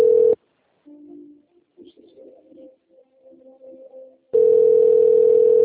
Ringback tone (North America), 440 + 480 Hz.